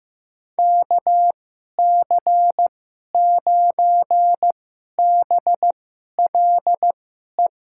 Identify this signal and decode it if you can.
Morse code, heard as a keyed tone, about 15 wpm, spaced as for 11 wpm, 700 Hz. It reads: KC9BLE
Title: USA Call Signs - 6 Characters - 2x Character Spacing - 15wpm